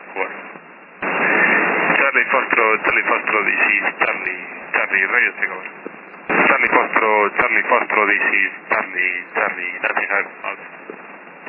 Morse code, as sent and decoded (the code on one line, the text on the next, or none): none